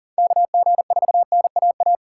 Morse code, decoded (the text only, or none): KG4NAA